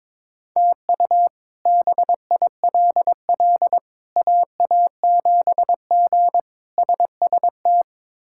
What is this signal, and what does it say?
Morse code, heard as a keyed tone, about 22 wpm, 700 Hz. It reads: TU BILL AA7G SST
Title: K1USN SST QSO Practice - 22wpm